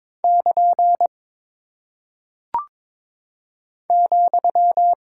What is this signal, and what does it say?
Morse code, heard as a keyed tone, about 22 wpm, 700 Hz. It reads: <KN> ,
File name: practice / 22wpm / Single Punctuation-Prosign - 22wpm